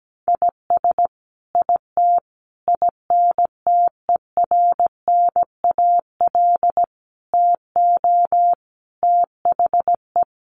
Morse code, heard as a keyed tone, about 17 wpm, 700 Hz. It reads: IS IT INTERNAL TO THE
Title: Sentences from Top 1400 Words 17wpm